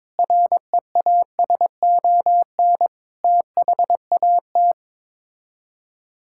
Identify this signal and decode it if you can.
Morse code, heard as a keyed tone, about 21 wpm, 700 Hz. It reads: REASON THAT
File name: practice / 22wpm / Sets of 6 Words - 22wpm